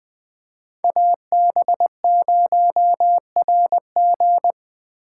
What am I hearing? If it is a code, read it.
Morse code, heard as a keyed tone, about 20 wpm, 700 Hz. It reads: AB0RG
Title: USA Call Signs - 5 Characters - 3x Character Spacing - 20wpm